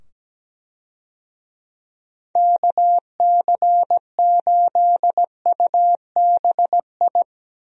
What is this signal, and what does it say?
Morse code, heard as a keyed tone, about 17 wpm, 700 Hz. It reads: KC8UBI
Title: USA Call Signs - 6 Characters - 17wpm